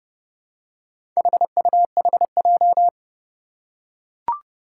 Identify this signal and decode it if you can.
Morse code, heard as a keyed tone, about 29 wpm, 700 Hz. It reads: HUHJ
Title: Four Challenging Letter-Number Combinations 30wpm